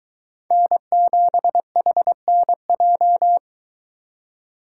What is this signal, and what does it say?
Morse code, heard as a keyed tone, about 23 wpm, 700 Hz. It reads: N7HNJ